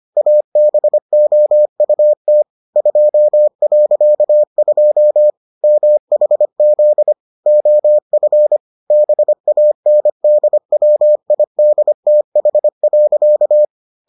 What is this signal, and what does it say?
Morse code, heard as a keyed tone, about 25 wpm, 600 Hz. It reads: ABOUT 2.2 MHZ OF BANDWIDTH.